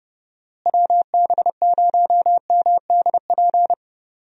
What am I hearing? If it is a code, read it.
Morse code, heard as a keyed tone, about 30 wpm, 700 Hz. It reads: WB0MDP
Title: USA Call Signs - 6 Characters - Repeated 3x - 30wpm